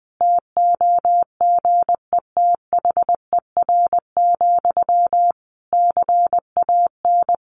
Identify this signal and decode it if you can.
Morse code, heard as a keyed tone, about 20 wpm, 700 Hz. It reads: TOGETHER, CAN
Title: Sentences from Top 1400 Words 20wpm